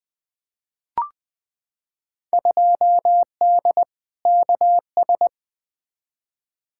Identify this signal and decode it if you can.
Morse code, heard as a keyed tone, about 20 wpm, 700 Hz. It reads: 2D KS
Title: Field Day Exchanges Encore - 20wpm